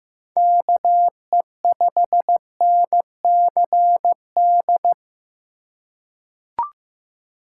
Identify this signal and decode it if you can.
Morse code, heard as a keyed tone, about 15 wpm, 700 Hz. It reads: KE5NCD
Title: USA Call Signs - 6 Characters - 3x Character Spacing - 15wpm